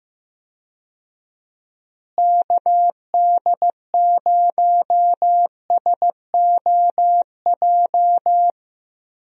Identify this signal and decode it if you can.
Morse code, heard as a keyed tone, about 15 wpm, 700 Hz. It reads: KD0SOJ
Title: USA Call Signs - 6 Characters - Repeated 2x - 15wpm